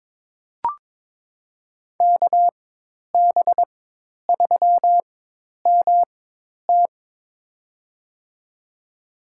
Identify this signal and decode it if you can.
Morse code, heard as a keyed tone, about 22 wpm, spaced as for 10 wpm, 700 Hz. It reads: KB3MT
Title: USA Call Signs - 5 Characters - 4x Character Spacing - 22wpm